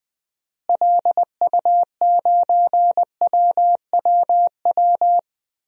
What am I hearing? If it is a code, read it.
Morse code, heard as a keyed tone, about 20 wpm, 700 Hz. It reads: LU9WWW
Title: DXCC-100 Call Signs - 6 Characters - Repeated 2x - 20wpm